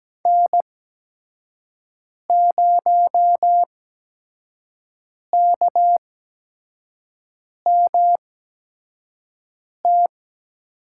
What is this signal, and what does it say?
Morse code, heard as a keyed tone, about 17 wpm, spaced as for 5 wpm, 700 Hz. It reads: N0KMT
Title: USA Call Signs - 5 Characters - 8x Character Spacing - 17wpm